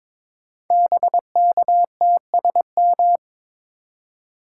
Morse code, heard as a keyed tone, about 22 wpm, 700 Hz. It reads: BKTSM
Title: Code Groups - 5 Letters - 22wpm